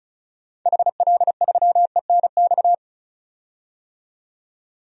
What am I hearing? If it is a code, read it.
Morse code, heard as a keyed tone, about 36 wpm, 700 Hz. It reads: HL3ENX
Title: DXCC-100 Call Signs - 6 Characters - 35wpm